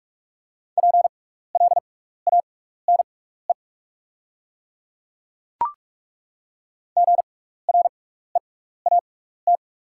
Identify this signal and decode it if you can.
Morse code, heard as a keyed tone, about 48 wpm, spaced as for 16 wpm, 700 Hz. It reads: PLANE GREAT
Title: 6x Character Spacing - Top 500 Words - 45wpm